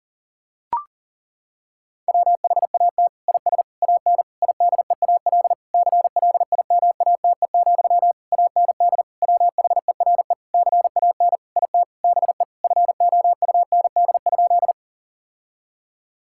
Morse code, heard as a keyed tone, about 40 wpm, 700 Hz. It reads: WHAT IS AN IDEAL CLIMATE, AND WHERE CAN IT BE FOUND?
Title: Sentences from Top 1600 Words 40wpm